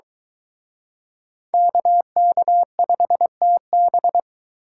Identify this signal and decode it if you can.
Morse code, heard as a keyed tone, about 23 wpm, 700 Hz. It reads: KK5TB